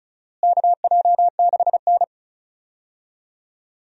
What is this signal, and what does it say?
Morse code, heard as a keyed tone, about 34 wpm, 700 Hz. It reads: KJ6N
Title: USA Call Signs - 4 Characters - Repeated 3x - 35wpm